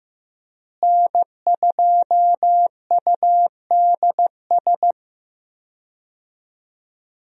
Morse code, heard as a keyed tone, about 15 wpm, 700 Hz. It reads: N2UDS
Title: USA Call Signs - 5 Characters - Repeated 2x - 15wpm